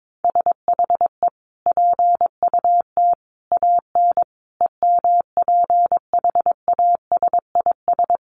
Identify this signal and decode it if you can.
Morse code, heard as a keyed tone, about 22 wpm, 700 Hz. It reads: SHE PUT AN EMPHASIS
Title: Sentences from Top 1400 Words 22wpm